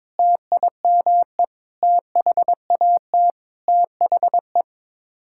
Morse code, heard as a keyed tone, about 22 wpm, 700 Hz. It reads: TIME THAT THE
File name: practice / 22wpm / Sets of 8 Words - 22wpm